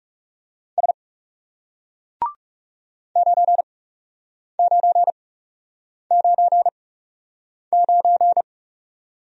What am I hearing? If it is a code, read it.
Morse code, heard as a keyed tone, about 38 wpm, spaced as for 8 wpm, 700 Hz. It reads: S 9999